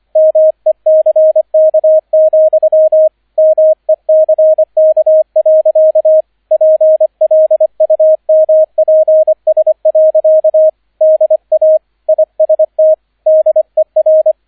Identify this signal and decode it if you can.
Morse code, heard as a keyed tone, about 25 wpm, 630 Hz, with radio noise. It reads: MECK, MECK. PLUMPS. DA IST DER
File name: 003 Dritter Streich 25wpm